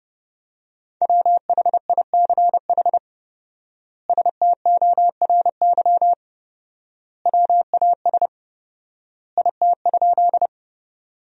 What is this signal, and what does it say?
Morse code, heard as a keyed tone, about 30 wpm, 700 Hz. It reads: WHICH STORY WAS IT?
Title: 4x Word-Spacing - Sentences from Top 300 Words - 30wpm